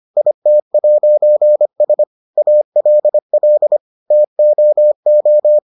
Morse code, heard as a keyed tone, about 25 wpm, 600 Hz. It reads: IT'S ALL TOO